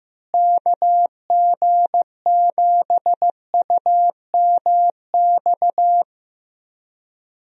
Morse code, heard as a keyed tone, about 14 wpm, 700 Hz. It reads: KG7UMX
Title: USA Call Signs - 6 Characters - Repeated 3x - 15wpm